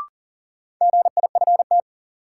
Morse code, heard as a keyed tone, about 38 wpm, 700 Hz. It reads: GIFT
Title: Sentences from Top 900 Words - New Word Drill - 40wpm